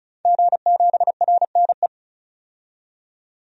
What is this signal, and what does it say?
Morse code, heard as a keyed tone, about 36 wpm, 700 Hz. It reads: G7RNE
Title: DXCC-100 Call Signs - 5 Characters - 35wpm